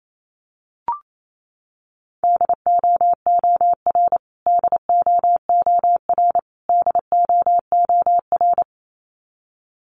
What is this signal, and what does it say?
Morse code, heard as a keyed tone, about 28 wpm, 700 Hz. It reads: DOOR DOOR DOOR
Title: Top 500 Words Repeated 28wpm